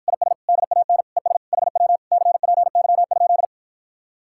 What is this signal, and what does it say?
Morse code, heard as a keyed tone, about 53 wpm, 700 Hz. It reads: IS DAN ES HW CPY?